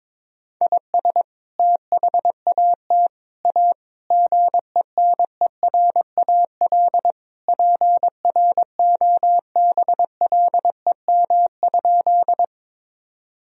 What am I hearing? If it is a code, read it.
Morse code, heard as a keyed tone, about 22 wpm, 700 Hz. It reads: IS THAT A GENERAL PROBLEM?
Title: Sentences from Top 1000 Words - Review Part 3 - 22wpm